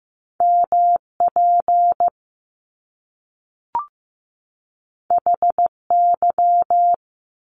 Morse code, heard as a keyed tone, about 15 wpm, 700 Hz. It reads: MP HY
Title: Top Two Letter Combinations in Words 15wpm